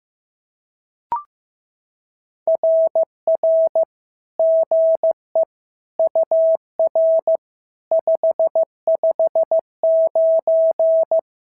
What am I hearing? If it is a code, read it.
Morse code, heard as a keyed tone, about 14 wpm, 650 Hz. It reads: RR GE UR 559
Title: SOTA QSO Practice 15wpm